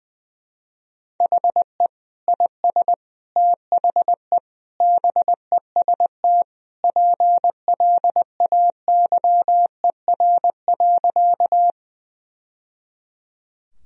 Morse code, heard as a keyed tone, about 21 wpm, 700 Hz. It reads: HE IS THE BEST PLAYER.
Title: Sentences from Top 1000 Words - Review Part 4 - 20wpm